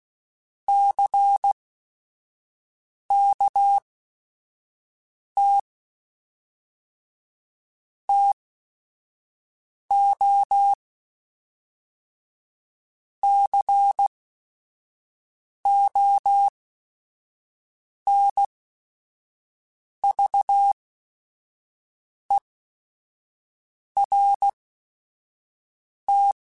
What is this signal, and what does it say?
Morse code, heard as a keyed tone, about 16 wpm, spaced as for 5 wpm, 800 Hz. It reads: CKT TO CONVERT
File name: Technical Subjects / Technical Subjects-mp3s / Sw Pwr Sup @5